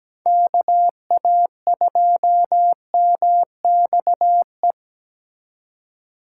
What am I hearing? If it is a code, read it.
Morse code, heard as a keyed tone, about 17 wpm, 700 Hz. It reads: KA2MXE
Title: USA Call Signs - 6 Characters - 3x Character Spacing - 17wpm